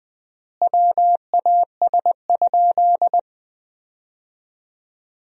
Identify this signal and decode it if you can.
Morse code, heard as a keyed tone, about 20 wpm, 700 Hz. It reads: WAS?
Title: 4x Word-Spacing - Sentences from Top 200 Words - 20wpm